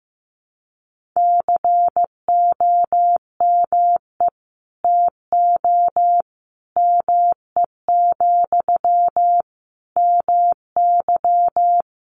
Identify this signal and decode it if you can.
Morse code, heard as a keyed tone, about 15 wpm, 700 Hz. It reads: COME TO ME, MY
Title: Sentences from Top 1500 Words 15wpm